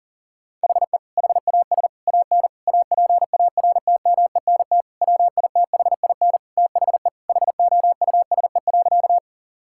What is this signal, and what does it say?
Morse code, heard as a keyed tone, about 40 wpm, 700 Hz. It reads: HE HAS AN APARTMENT WITHIN THE HOUSE.
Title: Sentences from Top 1000 Words - Review Part 6 - 40wpm